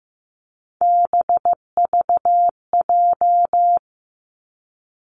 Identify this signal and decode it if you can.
Morse code, heard as a keyed tone, about 15 wpm, 700 Hz. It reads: BVJ
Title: Three Challenging Letter-Number Combinations 15wpm